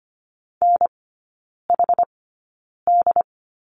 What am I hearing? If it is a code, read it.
Morse code, heard as a keyed tone, about 25 wpm, spaced as for 9 wpm, 700 Hz. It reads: NHD
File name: Lesson 14e - Number 2 - Three Characters 25wpm